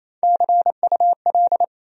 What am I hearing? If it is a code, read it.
Morse code, heard as a keyed tone, about 35 wpm, 700 Hz. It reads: CUL
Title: Speed-Racing Abbreviations - 28wpm